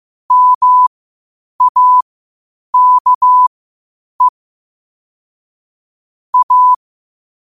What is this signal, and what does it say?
Morse code, heard as a keyed tone, about 15 wpm, spaced as for 8 wpm, 1000 Hz. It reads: MAKE A
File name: qotdfiles / 8wpm